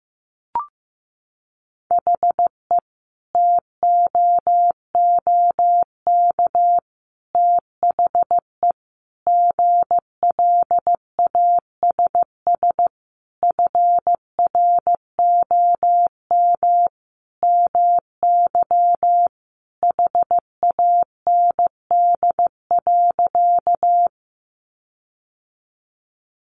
Morse code, heard as a keyed tone, about 15 wpm, 700 Hz. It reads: HE TOOK THE GLASS FROM MY HAND.
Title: Sentences from Top 1300 Words 15wpm